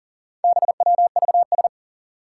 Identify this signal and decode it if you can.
Morse code, heard as a keyed tone, about 40 wpm, 700 Hz. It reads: BWVS
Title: Four Challenging Letter-Number Combinations 40wpm